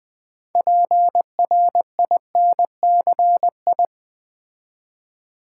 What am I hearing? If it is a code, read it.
Morse code, heard as a keyed tone, about 20 wpm, 700 Hz. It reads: PRINCI
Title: Build A Word - 3 Letters At A Time - 9-Letter Words - 20wpm